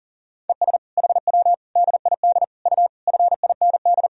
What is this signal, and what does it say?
Morse code, heard as a keyed tone, about 40 wpm, 700 Hz. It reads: ES HW DID U FIND
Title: Sentences from Top 200 Words and Abbreviations 40wpm